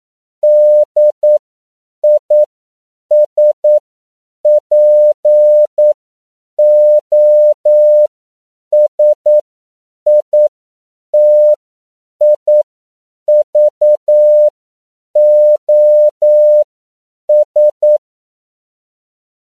Morse code, heard as a keyed tone, about 9 wpm, spaced as for 7 wpm, 600 Hz. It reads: DISPOSITIVOS
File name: CW / 10PPM